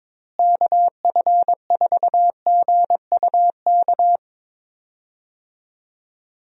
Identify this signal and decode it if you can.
Morse code, heard as a keyed tone, about 22 wpm, 700 Hz. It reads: KF4GUK
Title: USA Call Signs - 6 Characters - Repeated 2x - 22wpm